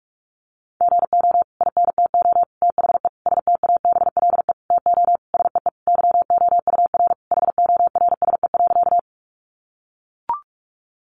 Morse code, heard as a keyed tone, about 44 wpm, 700 Hz. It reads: GO INTO THE STABLE TO SEE YOUR HORSE.
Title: Sentences from Top 1500 Words 45wpm